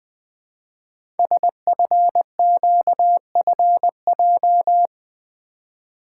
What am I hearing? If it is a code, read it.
Morse code, heard as a keyed tone, about 20 wpm, 700 Hz. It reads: SFQFJ
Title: Code Groups - 5 Letters - 20wpm